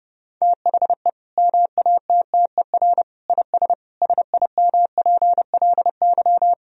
Morse code, heard as a keyed tone, about 31 wpm, 700 Hz. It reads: THE MATTER IS SIMPLY
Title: Sentences from Top 1000 Words - Review Part 4 - 30wpm